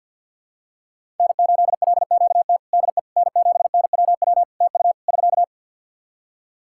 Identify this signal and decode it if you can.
Morse code, heard as a keyed tone, about 52 wpm, 700 Hz. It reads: N8LQT DE N7NWW TU <SK>
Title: Practice QSO - Exchange 4 - Ending - 50wpm